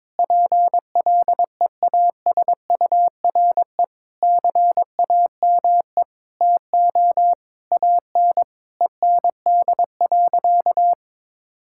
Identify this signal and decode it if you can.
Morse code, heard as a keyed tone, about 22 wpm, 700 Hz. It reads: PLEASURE CAME TO AN END.
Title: Sentences from Top 1600 Words 22wpm